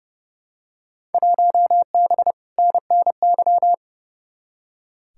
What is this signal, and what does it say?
Morse code, heard as a keyed tone, about 30 wpm, 700 Hz. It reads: 1B NNY
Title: Field Day Exchanges - 30wpm